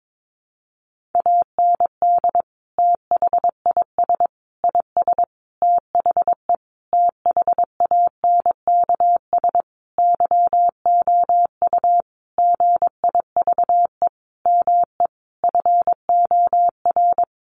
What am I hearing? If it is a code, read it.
Morse code, heard as a keyed tone, about 22 wpm, 700 Hz. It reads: AND THIS IS THE THANKS YOU GIVE ME FOR
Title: Sentences from Top 1200 Words 22wpm